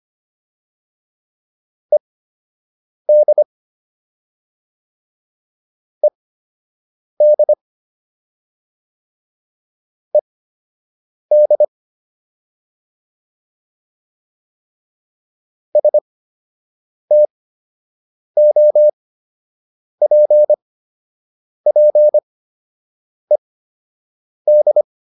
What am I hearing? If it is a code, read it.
Morse code, heard as a keyed tone, about 25 wpm, spaced as for 7 wpm, 600 Hz. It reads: ED ED ED STOPPED